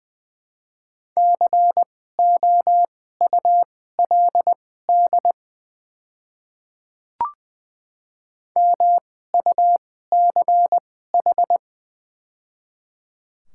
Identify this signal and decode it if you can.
Morse code, heard as a keyed tone, about 20 wpm, spaced as for 14 wpm, 700 Hz. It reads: COULD MUCH
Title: 2x Character Spacing - Top 300 Words - 20wpm